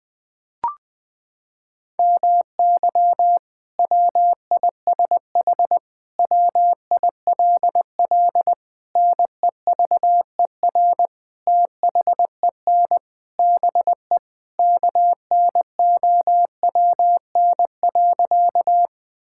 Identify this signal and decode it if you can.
Morse code, heard as a keyed tone, about 21 wpm, 700 Hz. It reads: MY WISH WILL NEVER THEN BE KNOWN.
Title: Sentences from Top 1600 Words 20wpm